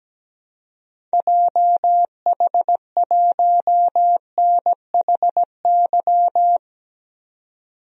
Morse code, heard as a keyed tone, about 17 wpm, 700 Hz. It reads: JH1NHY
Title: DXCC-100 Call Signs - 6 Characters - 17wpm